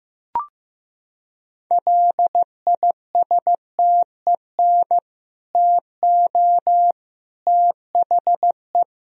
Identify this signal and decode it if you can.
Morse code, heard as a keyed tone, about 16 wpm, 700 Hz. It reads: LISTEN TO THE
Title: Sentences from Top 1000 Words - Review Part 3 - 15wpm